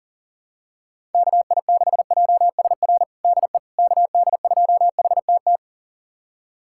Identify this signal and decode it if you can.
Morse code, heard as a keyed tone, about 40 wpm, 700 Hz. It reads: KI6JSR DE KD2HTT
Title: USA Callsign DE Callsign - 6 Characters - 40wpm